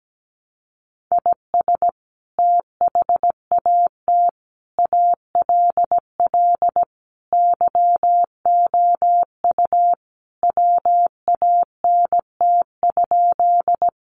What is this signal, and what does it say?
Morse code, heard as a keyed tone, about 17 wpm, 700 Hz. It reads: IS THAT ALL YOU WANT?
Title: Sentences from Top 200 Words 17wpm